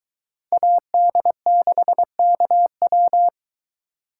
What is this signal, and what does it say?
Morse code, heard as a keyed tone, about 22 wpm, 700 Hz. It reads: AD6KW